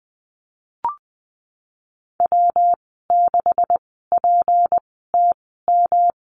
Morse code, heard as a keyed tone, about 19 wpm, spaced as for 14 wpm, 700 Hz. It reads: W6PTM
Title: 238 Lesson 38g - Letter q - Call Signs 20wpm